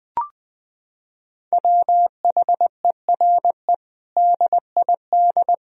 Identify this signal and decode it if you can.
Morse code, heard as a keyed tone, about 20 wpm, 700 Hz. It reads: WHERE DID
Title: Sentences from Top 400 Words and Abbreviations 20wpm